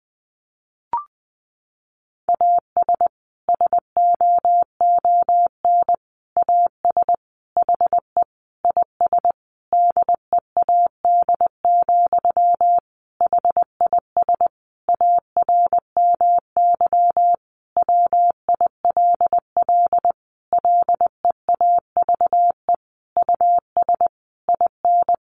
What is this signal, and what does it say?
Morse code, heard as a keyed tone, about 20 wpm, 700 Hz. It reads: AS SOON AS HE IS DEAD, HIS ARMY WILL LEAVE US IN
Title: Sentences from Top 1500 Words 20wpm